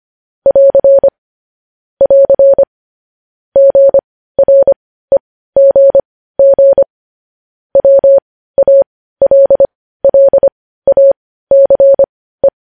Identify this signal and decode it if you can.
Morse code, heard as a keyed tone, about 25 wpm, spaced as for 15 wpm, 550 Hz. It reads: <AR> <AR> GREGG WALLACE
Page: News headlines at 15